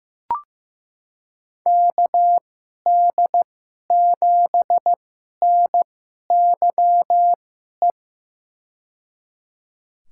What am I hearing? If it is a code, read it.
Morse code, heard as a keyed tone, about 15 wpm, spaced as for 11 wpm, 700 Hz. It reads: KD7NYE
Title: USA Call Signs - 6 Characters - 2x Character Spacing - 15wpm